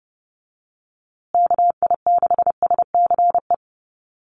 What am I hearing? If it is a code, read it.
Morse code, heard as a keyed tone, about 30 wpm, 700 Hz. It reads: KI6SCE
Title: USA Call Signs Repeated 2x 30wpm